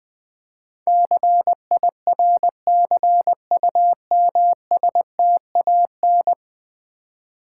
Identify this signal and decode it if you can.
Morse code, heard as a keyed tone, about 20 wpm, 700 Hz. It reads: CIRCUMSTAN
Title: Build A Word - 5 Letters At A Time - 12-Letter Words - 20wpm